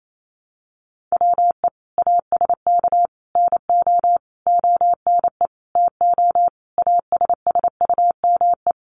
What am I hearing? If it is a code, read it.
Morse code, heard as a keyed tone, about 28 wpm, 700 Hz. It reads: WE ASK NO ONE TO ASSUME
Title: Sentences from Top 1500 Words 28wpm